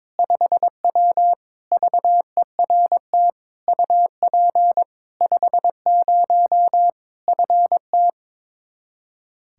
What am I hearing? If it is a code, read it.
Morse code, heard as a keyed tone, about 22 wpm, 700 Hz. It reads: 5W VERT UP 50 FT
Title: Practice QSO - Exchange 2 - Info - 22wpm